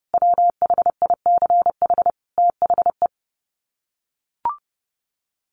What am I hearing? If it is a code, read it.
Morse code, heard as a keyed tone, about 30 wpm, 700 Hz. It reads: WHICH THE
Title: Sets of 5 Words Encore2 30wpm